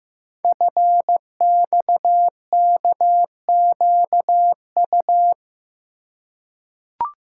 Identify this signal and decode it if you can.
Morse code, heard as a keyed tone, about 15 wpm, 700 Hz. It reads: FXKQU
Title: Code Groups - 5 Letters - 15wpm